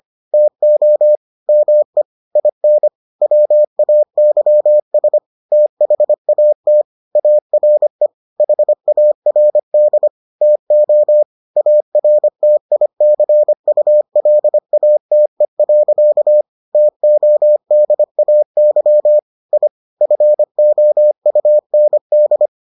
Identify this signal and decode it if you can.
Morse code, heard as a keyed tone, about 25 wpm, 600 Hz. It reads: TO ME IN WAYS THAT ARE HARD TO ARTICULATE. TODAY I FOUND